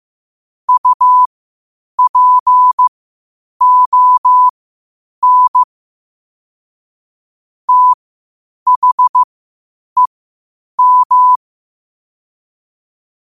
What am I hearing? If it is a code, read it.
Morse code, heard as a keyed tone, about 15 wpm, spaced as for 8 wpm, 1000 Hz. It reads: UPON THEM
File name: qotdfiles / 8wpm